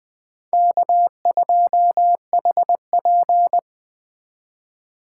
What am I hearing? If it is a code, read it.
Morse code, heard as a keyed tone, about 20 wpm, 700 Hz. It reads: K2HP
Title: Speed Racing - USA Call Signs 17wpm